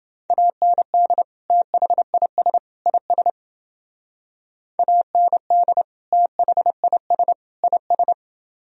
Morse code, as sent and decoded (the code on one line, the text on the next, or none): .- -. -.. / - .... .. ... / .. ... / .- -. -.. / - .... .. ... / .. ...
AND THIS IS AND THIS IS